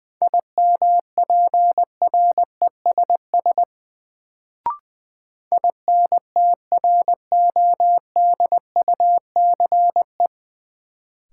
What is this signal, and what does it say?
Morse code, heard as a keyed tone, about 20 wpm, 700 Hz. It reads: IMPRESS INTRODUCE
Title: Sentences from Top 1600 Words - New Word Drill - 20wpm